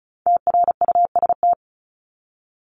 Morse code, heard as a keyed tone, about 34 wpm, 700 Hz. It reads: TRUST
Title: Top 1751-2000 Words 35wpm